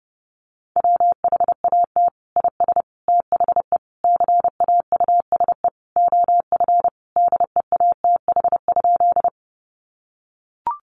Morse code, heard as a keyed tone, about 30 wpm, 700 Hz. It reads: WHAT IS THE CAUSE OF DEATH?
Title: Sentences from Top 1100 Words 30wpm